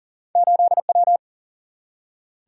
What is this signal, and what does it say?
Morse code, heard as a keyed tone, about 40 wpm, 700 Hz. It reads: 8W
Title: Single-Letter Single-Number 40wpm